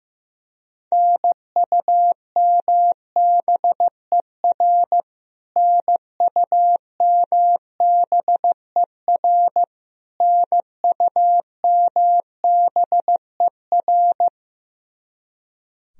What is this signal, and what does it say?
Morse code, heard as a keyed tone, about 15 wpm, 700 Hz. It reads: NUMBER NUMBER NUMBER
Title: Top 100 Words Repeated 15wpm